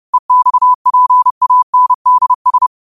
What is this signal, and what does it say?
Morse code, heard as a keyed tone, about 30 wpm, 1000 Hz. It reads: EXPANDS
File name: qotdfiles / 30wpm